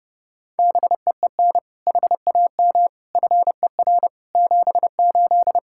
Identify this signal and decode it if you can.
Morse code, heard as a keyed tone, about 30 wpm, 700 Hz. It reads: BEEN HAM FER 78
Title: Practice QSO - Exchange 3 - More Info - 30wpm